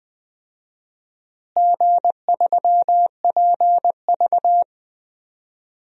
Morse code, heard as a keyed tone, about 21 wpm, 700 Hz. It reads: G3PV
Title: DXCC-100 Call Signs - 4 Characters - Repeated 3x - 20wpm